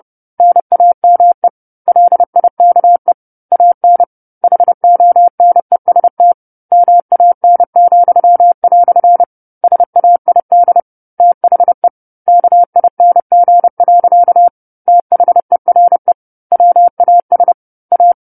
Morse code, heard as a keyed tone, about 30 wpm, 700 Hz. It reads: NAME LIKE AN HONEST MAN," SAID THE KING. THERE WAS A
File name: 10-Chapter-10 30 WPM